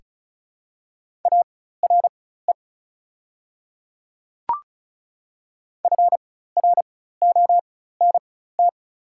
Morse code, heard as a keyed tone, about 34 wpm, spaced as for 16 wpm, 690 Hz. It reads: ARE FRONT
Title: 4x Character Spacing - Top 500 Words - 35wpm